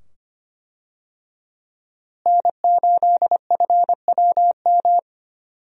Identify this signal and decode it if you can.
Morse code, heard as a keyed tone, about 25 wpm, 700 Hz. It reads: N8FWM
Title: USA Call Signs - 5 Characters - Repeated 2x - 25wpm